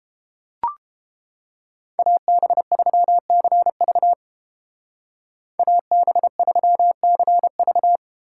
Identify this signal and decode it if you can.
Morse code, heard as a keyed tone, about 29 wpm, 700 Hz. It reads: AB3CV AB3CV